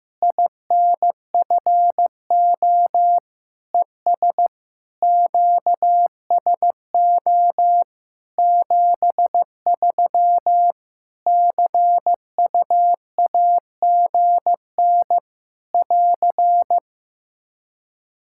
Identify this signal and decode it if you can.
Morse code, heard as a keyed tone, about 15 wpm, 700 Hz. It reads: INFO ES QSO 73 CUAGN <AR>
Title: Practice QSO - Exchange 4 - Ending - 15wpm